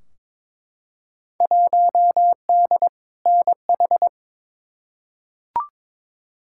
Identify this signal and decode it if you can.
Morse code, heard as a keyed tone, about 22 wpm, 700 Hz. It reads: 1D NH
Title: Field Day Exchanges - 22wpm